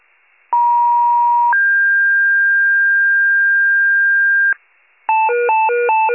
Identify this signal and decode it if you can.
Two-tone page, heard as a one-second tone, then a three-second tone, about 950 then 1600 Hz, with noise. It is heard from 0.5 to 4.5 s.
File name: pager / 83 RESCUE DUTY / wcalert